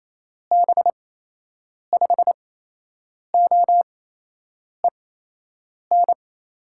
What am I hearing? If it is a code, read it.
Morse code, heard as a keyed tone, about 28 wpm, 700 Hz. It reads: B 5 O E N
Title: USA Call Signs - 6 Characters - 8x Character Spacing - 28wpm